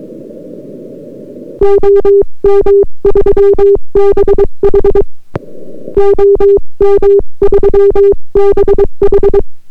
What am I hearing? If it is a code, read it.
Morse code, heard as a keyed tone, about 22 wpm, 400 Hz. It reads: OM3BH OM3BH